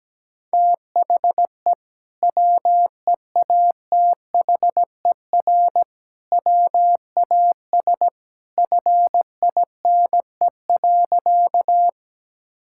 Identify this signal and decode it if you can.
Morse code, heard as a keyed tone, about 17 wpm, 700 Hz. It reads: THE WEATHER WAS FINE.
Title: Sentences from Top 1000 Words - Review Part 5 - 17wpm